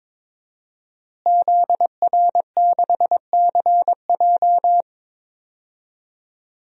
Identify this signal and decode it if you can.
Morse code, heard as a keyed tone, about 22 wpm, 700 Hz. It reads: ZR6CJ
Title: DXCC-100 Call Signs - 5 Characters - Repeated 2x - 22wpm